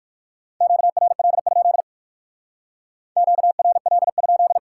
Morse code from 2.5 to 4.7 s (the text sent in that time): QRL?